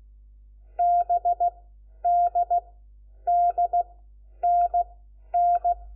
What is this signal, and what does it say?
Morse code, heard as a keyed tone, about 16 wpm, spaced as for 10 wpm, 700 Hz. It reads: BDDNN
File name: Lecciones / CLASE Nº4